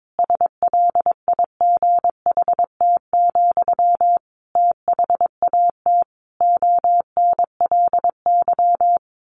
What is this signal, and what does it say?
Morse code, heard as a keyed tone, about 22 wpm, 700 Hz. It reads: SLIGHT, THAT ONLY
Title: A Sound Of Thunder 22wpm